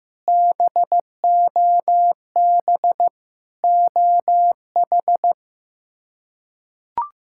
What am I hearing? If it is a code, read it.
Morse code, heard as a keyed tone, about 15 wpm, 700 Hz. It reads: BOB OH
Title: K1USN SST QSO Practice - 15wpm